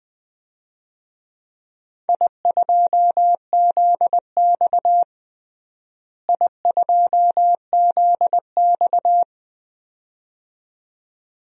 Morse code, heard as a keyed tone, about 20 wpm, 700 Hz. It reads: I2ZX I2ZX
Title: DXCC-100 Call Signs - 4 Characters - Repeated 2x - 20wpm